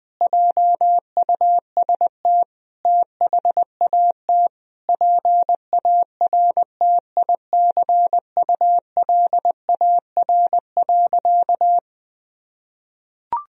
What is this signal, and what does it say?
Morse code, heard as a keyed tone, about 21 wpm, 700 Hz. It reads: JUST THAT PARTICULAR.
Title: Sentences from Top 1000 Words - Review Part 4 - 20wpm